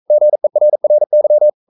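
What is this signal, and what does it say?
Morse code, heard as a keyed tone, about 42 wpm, 600 Hz. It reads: GERRY
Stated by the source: Morse code sending of an English novel at 42 wpm.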